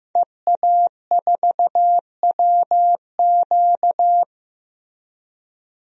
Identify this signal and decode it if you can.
Morse code, heard as a keyed tone, about 15 wpm, 700 Hz. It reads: EA4WQ
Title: DXCC-100 Call Signs - 5 Characters - 15wpm